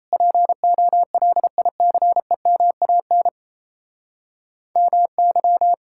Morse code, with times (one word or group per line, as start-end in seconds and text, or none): 0.1-3.3 POLICEMAN
4.8-5.7 MY